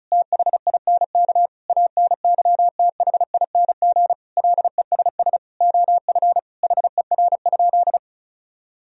Morse code, heard as a keyed tone, about 34 wpm, 700 Hz. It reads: THINK ANYTHING LESS OF HER?
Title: Sentences from Top 1000 Words - Review Part 2 - 35wpm